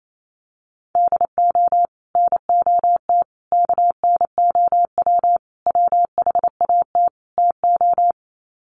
Morse code, heard as a keyed tone, about 28 wpm, 700 Hz. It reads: DO NOT KNOW WHAT TO
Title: Sets of 5 Words 28wpm